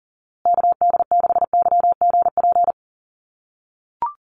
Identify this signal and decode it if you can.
Morse code, heard as a keyed tone, about 40 wpm, 700 Hz. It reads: KD6YGP
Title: USA Call Signs Repeated 3x 40wpm